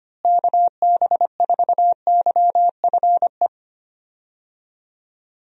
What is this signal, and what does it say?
Morse code, heard as a keyed tone, about 25 wpm, 700 Hz. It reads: KB4YFE
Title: USA Call Signs - 6 Characters - Repeated 2x - 25wpm